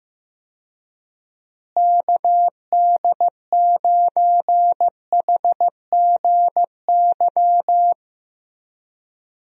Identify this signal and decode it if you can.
Morse code, heard as a keyed tone, about 15 wpm, 700 Hz. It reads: KD9HGY
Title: USA Call Signs - 6 Characters - Repeated 2x - 15wpm